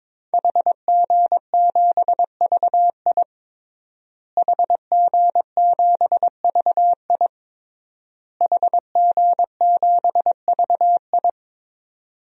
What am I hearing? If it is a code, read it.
Morse code, heard as a keyed tone, about 22 wpm, 700 Hz. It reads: HG7VI HG7VI HG7VI
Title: DXCC-100 Call Signs - 5 Characters - Repeated 3x - 22wpm